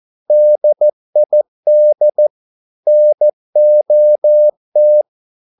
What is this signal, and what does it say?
Morse code, heard as a keyed tone, about 14 wpm, 600 Hz. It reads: DID NOT